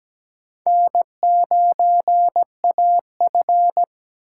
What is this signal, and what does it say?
Morse code, heard as a keyed tone, about 20 wpm, 700 Hz. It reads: N9AF
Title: Speed Racing - USA 1x2 Call Signs 17wpm